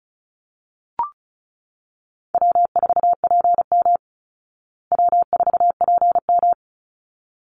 Morse code, heard as a keyed tone, about 34 wpm, 700 Hz. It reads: W4PM W4PM
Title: Call Signs Repeated 3x - 35wpm